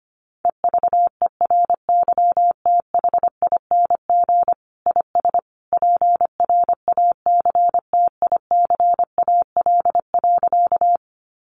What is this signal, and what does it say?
Morse code, heard as a keyed tone, about 25 wpm, 700 Hz. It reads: EVERYTHING IS PRACTICAL.
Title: Sentences from Top 1400 Words 25wpm